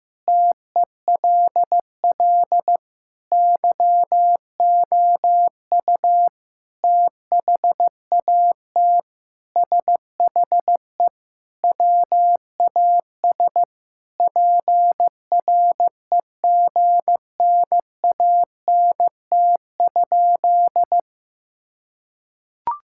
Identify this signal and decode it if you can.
Morse code, heard as a keyed tone, about 15 wpm, 700 Hz. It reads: TELL YOU THAT SHE WAS PREGNANT?
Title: Sentences from Top 1000 Words - Review Part 6 - 15wpm